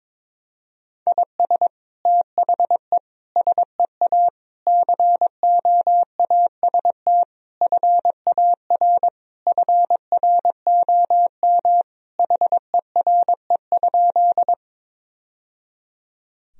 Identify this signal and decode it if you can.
Morse code, heard as a keyed tone, about 22 wpm, 700 Hz. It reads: IS THE SEA COAST FAR FROM HERE?
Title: Sentences from Top 1000 Words - Review Part 4 - 22wpm